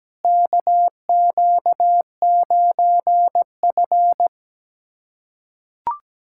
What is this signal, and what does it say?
Morse code, heard as a keyed tone, about 17 wpm, 700 Hz. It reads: KQ9F
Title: USA Callsign DE Callsign - 4 Characters - 17wpm